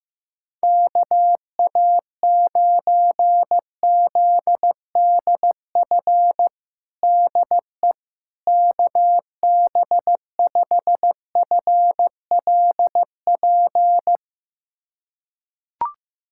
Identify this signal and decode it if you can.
Morse code, heard as a keyed tone, about 15 wpm, 700 Hz. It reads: KA9ZDF DE KB5FLP
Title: USA Callsign DE Callsign - 6 Characters - 15wpm